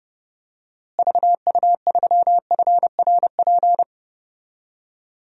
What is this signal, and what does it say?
Morse code, heard as a keyed tone, about 30 wpm, 700 Hz. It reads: VU3FRP
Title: DXCC-100 Call Signs - 6 Characters - Repeated 3x - 30wpm